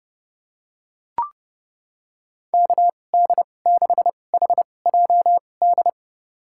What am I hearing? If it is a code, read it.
Morse code, heard as a keyed tone, about 30 wpm, spaced as for 22 wpm, 700 Hz. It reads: KD6HJD
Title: USA Call Signs - 6 Characters - 2x Character Spacing - 30wpm